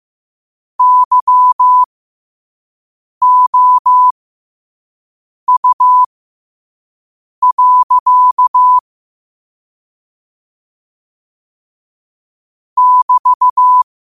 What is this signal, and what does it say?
Morse code, heard as a keyed tone, about 15 wpm, spaced as for 5 wpm, 1000 Hz. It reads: YOU. <BT>
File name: qotdfiles / 5wpm